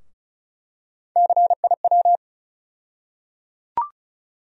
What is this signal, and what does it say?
Morse code, heard as a keyed tone, about 34 wpm, 700 Hz. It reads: CIW
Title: USA Call Sign Postfixes - 3 Letters 35wpm